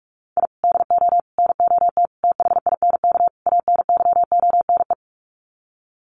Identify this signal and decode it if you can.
Morse code, heard as a keyed tone, about 44 wpm, 700 Hz. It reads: I DO NOT THINK ANYONE
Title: Sets of 5 Words Encore 45wpm